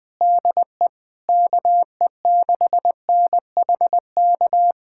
Morse code, heard as a keyed tone, about 21 wpm, 700 Hz. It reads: DE KE6NHK
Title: USA Callsign DE Callsign 20wpm